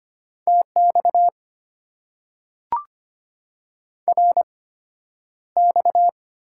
Morse code, heard as a keyed tone, about 25 wpm, spaced as for 11 wpm, 700 Hz. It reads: TX RX
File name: Top 90 QSO Elements - 8x Character Spacing - 25wpm